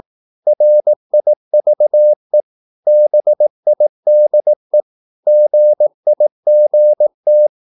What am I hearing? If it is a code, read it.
Morse code, heard as a keyed tone, about 18 wpm, 600 Hz. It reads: RIVE BIDE GIGT
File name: Korte ord DK 18wpm